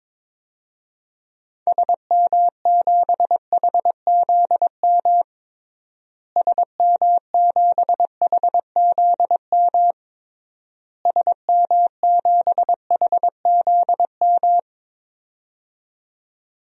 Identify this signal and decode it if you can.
Morse code, heard as a keyed tone, about 22 wpm, 700 Hz. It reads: SM7HZM SM7HZM SM7HZM
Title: DXCC-100 Call Signs - 6 Characters - Repeated 3x - 22wpm